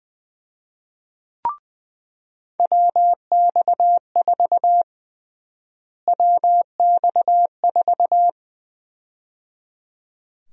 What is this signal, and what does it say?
Morse code, heard as a keyed tone, about 20 wpm, 700 Hz. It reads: WX4 WX4
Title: USA Call Sign Prefixes - Repeated 2x - 20wpm